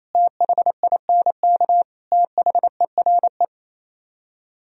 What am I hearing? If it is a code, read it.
Morse code, heard as a keyed tone, about 35 wpm, 700 Hz. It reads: THINK THERE
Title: Speed-Racing - Sets of 5 Words 28wpm